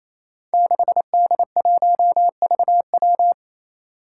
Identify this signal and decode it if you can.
Morse code, heard as a keyed tone, about 28 wpm, 700 Hz. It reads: 6D1VW
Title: Five Challenging Letter-Number Combinations 28wpm